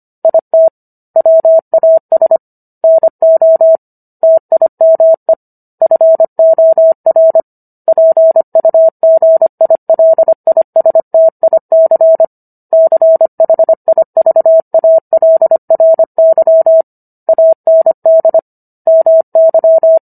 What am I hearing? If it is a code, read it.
Morse code, heard as a keyed tone, about 25 wpm, 650 Hz. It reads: IT WAS NO TIME FOR PUGILISTIC CHIVALRY AND MY